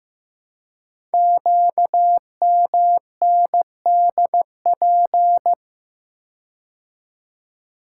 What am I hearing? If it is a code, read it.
Morse code, heard as a keyed tone, about 15 wpm, 700 Hz. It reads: QMNDP
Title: Code Groups - 5 Letters - 15wpm